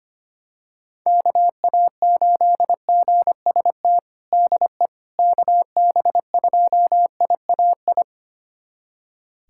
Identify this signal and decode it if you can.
Morse code, heard as a keyed tone, about 25 wpm, 700 Hz. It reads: KA8GST DE KB2IAI
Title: USA Callsign DE Callsign - 6 Characters - 25wpm